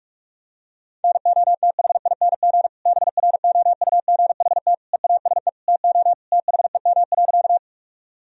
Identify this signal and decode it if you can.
Morse code, heard as a keyed tone, about 46 wpm, 700 Hz. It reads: NOTHING BROUGHT EASE TO THEM.
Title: Sentences from Top 1000 Words - Review Part 1 - 45wpm